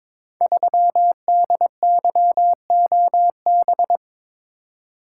Morse code, heard as a keyed tone, about 22 wpm, 700 Hz. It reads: 3DYOB
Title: Code Groups - 5 Letter:Numbers - 22wpm